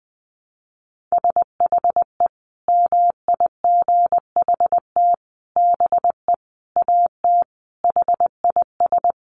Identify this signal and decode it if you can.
Morse code, heard as a keyed tone, about 20 wpm, 700 Hz. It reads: SHE MIGHT BE AT HIS
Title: Sentences from Top 300 Words 20wpm